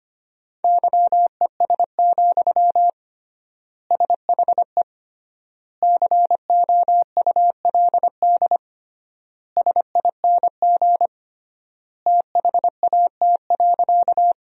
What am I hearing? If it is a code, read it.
Morse code, heard as a keyed tone, about 25 wpm, 700 Hz. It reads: YES, SHE COULD SING THAT.
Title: 3x Word-Spacing - Sentences from Top 500 Words - 25wpm